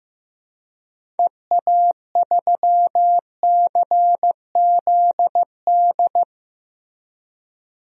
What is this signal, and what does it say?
Morse code, heard as a keyed tone, about 15 wpm, 700 Hz. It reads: EA3CZD
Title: DXCC-100 Call Signs - 6 Characters - 15wpm